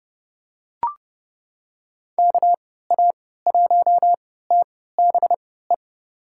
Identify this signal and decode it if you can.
Morse code, heard as a keyed tone, about 30 wpm, spaced as for 17 wpm, 700 Hz. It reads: KA1TBE
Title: USA Call Signs - 6 Characters - 3x Character Spacing - 30wpm